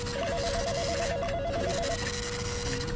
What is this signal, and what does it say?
There is no code.